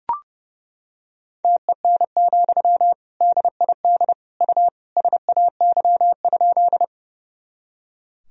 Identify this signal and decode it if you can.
Morse code, heard as a keyed tone, about 30 wpm, 700 Hz. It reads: TEN, DID U SAY?
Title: Sentences from Top 400 Words and Abbreviations 30wpm